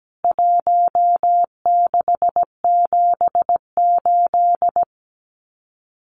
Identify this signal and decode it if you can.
Morse code, heard as a keyed tone, about 17 wpm, 700 Hz. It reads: 1678
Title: CWT Exchange 17wpm